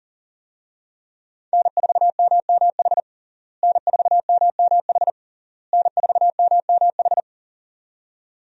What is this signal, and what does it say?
Morse code, heard as a keyed tone, about 40 wpm, 700 Hz. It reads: N4MMH N4MMH N4MMH
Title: USA Call Signs - 5 Characters - Repeated 3x - 40wpm